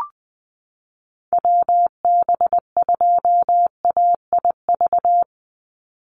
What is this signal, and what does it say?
Morse code, heard as a keyed tone, about 20 wpm, 700 Hz. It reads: WB2AIV
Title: Call Signs - 20wpm